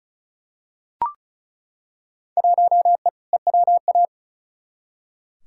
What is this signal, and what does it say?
Morse code, heard as a keyed tone, about 36 wpm, 700 Hz. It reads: 1E EWA
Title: Field Day Exchanges - 35wpm